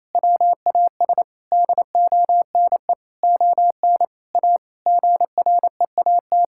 Morse code, heard as a keyed tone, about 28 wpm, 700 Hz. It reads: WAS DONE ON A GREAT
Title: Sentences from Top 1000 Words - Review Part 5 - 28wpm